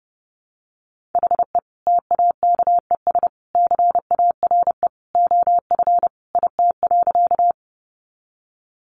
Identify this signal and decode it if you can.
Morse code, heard as a keyed tone, about 30 wpm, 700 Hz. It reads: HE TAKES CARE OF IT.
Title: Additional Easy Phrases 30wpm